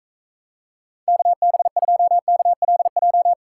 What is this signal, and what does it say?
Morse code, heard as a keyed tone, about 45 wpm, 700 Hz. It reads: KB2KLJ